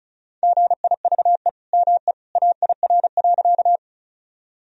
Morse code, heard as a keyed tone, about 36 wpm, 690 Hz. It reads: GIVE ME AIR.
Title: Sentences from Top 1000 Words - Review Part 1 - 35wpm